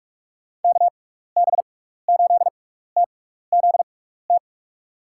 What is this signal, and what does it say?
Morse code, heard as a keyed tone, about 46 wpm, spaced as for 15 wpm, 700 Hz. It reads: KB8TZT
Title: USA Call Signs - 6 Characters - 6x Character Spacing - 45wpm